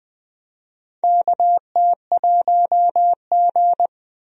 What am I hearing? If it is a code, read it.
Morse code, heard as a keyed tone, about 20 wpm, 700 Hz. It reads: KT1G
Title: USA Call Signs - 4 Characters - 20wpm